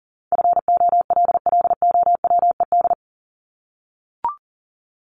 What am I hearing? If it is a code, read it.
Morse code, heard as a keyed tone, about 40 wpm, 700 Hz. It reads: FOLLOWED
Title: ED Suffix - Top 100 Words with Base Words 40wpm